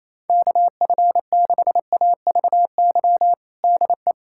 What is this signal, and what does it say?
Morse code, heard as a keyed tone, about 28 wpm, 700 Hz. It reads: KF6AVY DE
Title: USA Callsign DE Callsign - 6 Characters - 28wpm